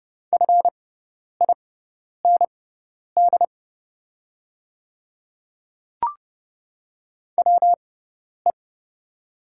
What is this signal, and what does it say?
Morse code, heard as a keyed tone, about 30 wpm, spaced as for 10 wpm, 690 Hz. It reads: FIND WE
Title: 6x Character Spacing - Top 100 Words - 30wpm